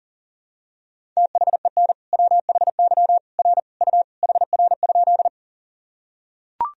Morse code, heard as a keyed tone, about 37 wpm, 700 Hz. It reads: THEN WHY R U HR?
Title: Sentences from Top 200 Words and Abbreviations 40wpm